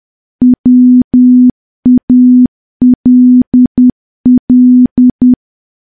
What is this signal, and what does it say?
Morse code, heard as a keyed tone, about 10 wpm, 250 Hz. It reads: WALL
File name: morse / morse-training-10WPM